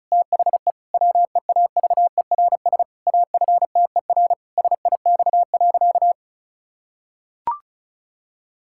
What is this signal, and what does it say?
Morse code, heard as a keyed tone, about 36 wpm, 700 Hz. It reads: THE WEAVERS AFTER SIX.
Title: Pangram Challenge v2 - 35wpm